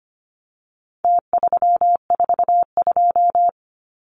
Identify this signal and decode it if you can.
Morse code, heard as a keyed tone, about 25 wpm, 700 Hz. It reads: T342
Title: Numbers - Contest Style - 4-Digit 25wpm